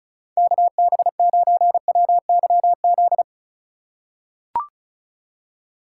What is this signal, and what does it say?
Morse code, heard as a keyed tone, about 36 wpm, 700 Hz. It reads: KB9WYZ
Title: USA Call Signs - 6 Characters - 2x Character Spacing - 35wpm